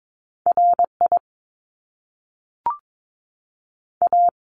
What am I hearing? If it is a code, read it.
Morse code, heard as a keyed tone, about 22 wpm, 700 Hz. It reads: RI A
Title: Top Two Letter Combinations in Words 22wpm